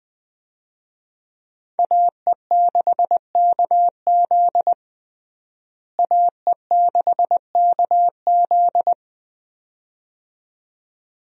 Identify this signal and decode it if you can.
Morse code, heard as a keyed tone, about 20 wpm, 700 Hz. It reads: AE6KZ AE6KZ
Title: USA Call Signs - 5 Characters - Repeated 2x - 20wpm